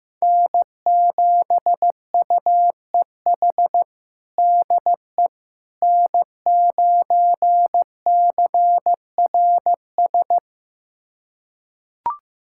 Morse code, heard as a keyed tone, about 15 wpm, 700 Hz. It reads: N7UEH DE N9CRS
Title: USA Callsign DE Callsign - 5 Characters - 15wpm